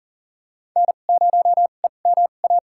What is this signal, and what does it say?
Morse code, heard as a keyed tone, about 40 wpm, spaced as for 29 wpm, 700 Hz. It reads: N0EMA
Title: USA Call Signs - 5 Characters - 2x Character Spacing - 40wpm